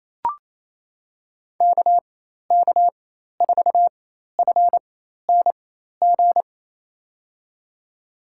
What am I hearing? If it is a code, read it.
Morse code, heard as a keyed tone, about 28 wpm, spaced as for 13 wpm, 700 Hz. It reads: KK4FNG
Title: USA Call Signs - 4x Character Spacing - 28wpm